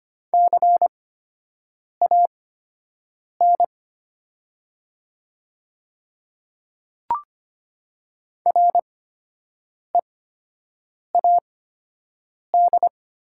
Morse code, heard as a keyed tone, about 25 wpm, spaced as for 7 wpm, 700 Hz. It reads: CAN READ
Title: 8x Character Spacing - Top 200 Words - 25wpm